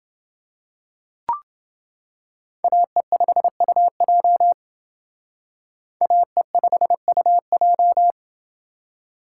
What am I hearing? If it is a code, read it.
Morse code, heard as a keyed tone, about 28 wpm, 700 Hz. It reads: AE5UJ AE5UJ